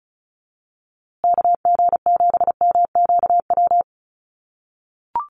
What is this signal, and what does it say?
Morse code, heard as a keyed tone, about 34 wpm, 700 Hz. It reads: KG7MQW
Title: USA Call Signs Repeated 2x 35wpm